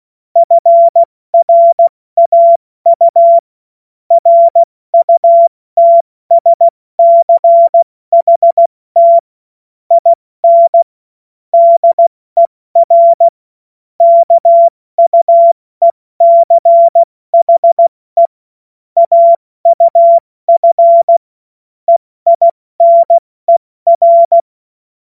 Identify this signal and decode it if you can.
Morse code, heard as a keyed tone, about 16 wpm, spaced as for 14 wpm, 680 Hz. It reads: FRAU RUTSCHT IN DER KUECHE AUF EINER
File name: Morsekurs / Witze / Witz 02 April 2024 70 F80